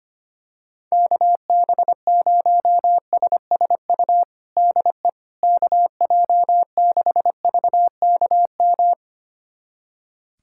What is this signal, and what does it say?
Morse code, heard as a keyed tone, about 25 wpm, 700 Hz. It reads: KB0SSU DE KJ6VKM
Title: USA Callsign DE Callsign - 6 Characters - 25wpm